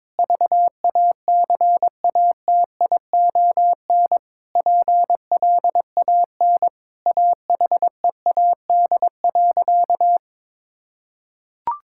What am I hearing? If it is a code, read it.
Morse code, heard as a keyed tone, about 21 wpm, 700 Hz. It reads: VACATION PLAN AHEAD.
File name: Sentences with 7-Word Phrases - 22wpm